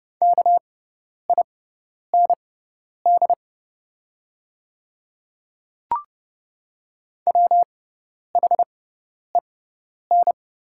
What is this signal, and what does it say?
Morse code, heard as a keyed tone, about 32 wpm, spaced as for 10 wpm, 700 Hz. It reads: KIND WHEN
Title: 6x Character Spacing - Top 500 Words - 30wpm